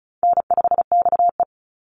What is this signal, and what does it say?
Morse code, heard as a keyed tone, about 34 wpm, 700 Hz. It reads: N5XE
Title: Easy Call Signs Repeated 2x - 35wpm